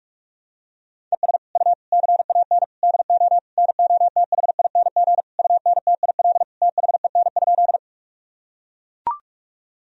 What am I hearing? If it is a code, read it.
Morse code, heard as a keyed tone, about 41 wpm, 700 Hz. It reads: ES U CAN DO NOTHING UNTIL THEN?
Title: Sentences from Top 500 Words and Abbreviations 45wpm